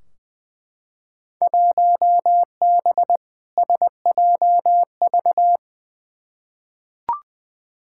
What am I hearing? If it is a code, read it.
Morse code, heard as a keyed tone, about 20 wpm, 700 Hz. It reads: 1B SJV
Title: Field Day Exchanges Encore - 20wpm